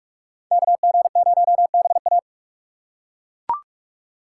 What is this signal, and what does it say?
Morse code, heard as a keyed tone, about 47 wpm, 690 Hz. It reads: KG0BA
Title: USA Call Signs - 5 Characters - 2x Character Spacing - 45wpm